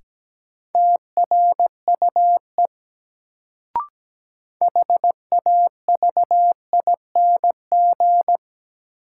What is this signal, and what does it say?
Morse code, heard as a keyed tone, about 17 wpm, 700 Hz. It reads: TRUE HAVING
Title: Sets of 5 Words Encore - New Word Drill - 17wpm